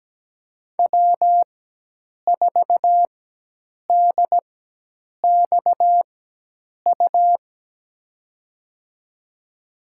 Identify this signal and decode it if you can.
Morse code, heard as a keyed tone, about 17 wpm, spaced as for 8 wpm, 700 Hz. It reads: W4DXU
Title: USA Call Signs - 5 Characters - 4x Character Spacing - 17wpm